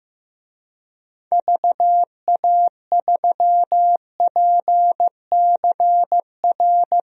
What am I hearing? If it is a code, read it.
Morse code, heard as a keyed tone, about 15 wpm, 700 Hz. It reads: VA3PCR
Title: DXCC-100 Call Signs - 6 Characters - Repeated 3x - 15wpm